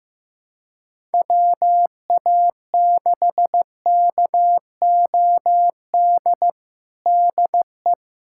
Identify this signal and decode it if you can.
Morse code, heard as a keyed tone, about 15 wpm, 700 Hz. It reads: WA6KOD DE
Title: USA Callsign DE Callsign - 6 Characters - 15wpm